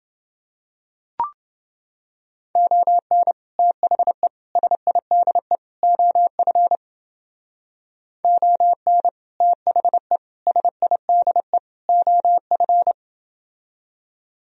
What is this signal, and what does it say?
Morse code, heard as a keyed tone, about 30 wpm, 700 Hz. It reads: ON THE SIDE OF ON THE SIDE OF